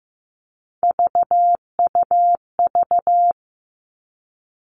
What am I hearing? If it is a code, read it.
Morse code, heard as a keyed tone, about 15 wpm, 700 Hz. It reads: VUV
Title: Three Challenging Letter-Number Combinations 15wpm